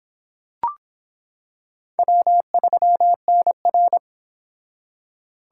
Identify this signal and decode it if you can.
Morse code, heard as a keyed tone, about 25 wpm, 700 Hz. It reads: W3NR